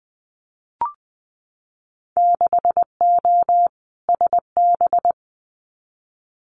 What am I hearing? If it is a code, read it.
Morse code, heard as a keyed tone, about 20 wpm, 700 Hz. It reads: 6O SB
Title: Winter Field Day Exchanges Encore 20wpm